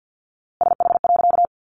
Morse code, heard as a keyed tone, about 50 wpm, 700 Hz. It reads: SH.
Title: A Sound Of Thunder 50wpm